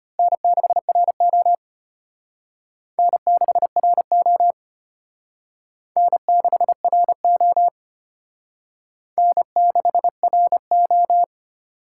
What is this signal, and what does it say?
Morse code, heard as a keyed tone, about 32 wpm, 700 Hz. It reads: N6RO N6RO N6RO N6RO